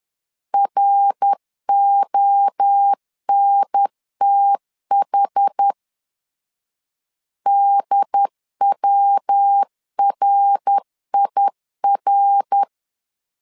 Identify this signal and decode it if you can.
Morse code, heard as a keyed tone, about 10 wpm, 800 Hz. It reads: RONTH DWRIR
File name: CW / DET / 1 3 1500C S B 10wpm 3335